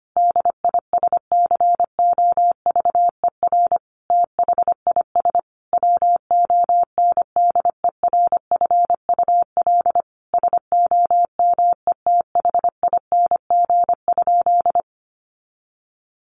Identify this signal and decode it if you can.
Morse code, heard as a keyed tone, about 25 wpm, 700 Hz. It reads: DISCOVER THIS WONDERFUL SOMETHING?
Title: Sentences from Top 1100 Words 25wpm